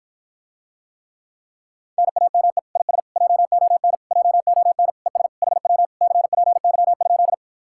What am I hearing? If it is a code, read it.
Morse code, heard as a keyed tone, about 53 wpm, 700 Hz. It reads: NAME IS JON JON ES HW CPY?